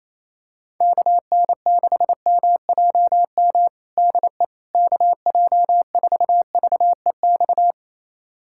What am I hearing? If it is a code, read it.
Morse code, heard as a keyed tone, about 28 wpm, 700 Hz. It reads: KN6MJM DE KJ4VEX
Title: USA Callsign DE Callsign - 6 Characters - 28wpm